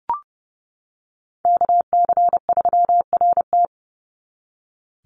Morse code, heard as a keyed tone, about 30 wpm, 700 Hz. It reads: KC3RT
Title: NA SOTA Chaser Callsigns 30wpm